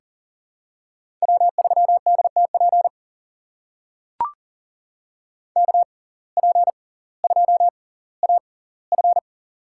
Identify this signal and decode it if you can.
Morse code, heard as a keyed tone, about 40 wpm, spaced as for 20 wpm, 700 Hz. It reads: W3DTP KP2AF
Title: USA Call Signs - 5 Characters - 6x Character Spacing - 40wpm